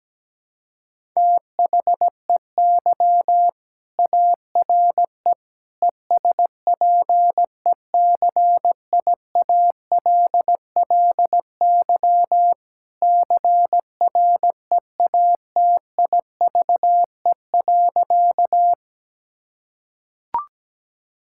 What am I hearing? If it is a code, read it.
Morse code, heard as a keyed tone, about 17 wpm, 700 Hz. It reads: THEY ARE ESPECIALLY CREATIVE.
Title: Sentences from Top 1000 Words - Review Part 5 - 17wpm